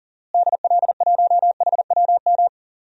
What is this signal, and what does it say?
Morse code, heard as a keyed tone, about 40 wpm, 700 Hz. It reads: DL1HWM
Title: DXCC-100 Call Signs - 6 Characters - Repeated 3x - 40wpm